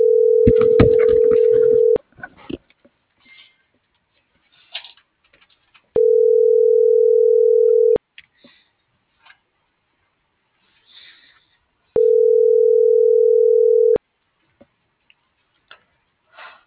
Ringback tone (North America), 440 + 480 Hz.
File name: CENTRAL GENERAL / out zabdi.escobedo